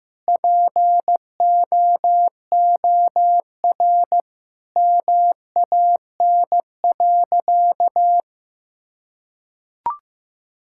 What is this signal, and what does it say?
Morse code, heard as a keyed tone, about 15 wpm, 700 Hz. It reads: POOR MAN.
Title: Sentences from Top 1000 Words - Review Part 5 - 15wpm